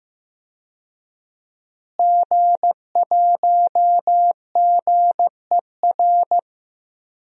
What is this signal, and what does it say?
Morse code, heard as a keyed tone, about 15 wpm, 700 Hz. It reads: G1GER
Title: DXCC-100 Call Signs - 5 Characters - Repeated 3x - 15wpm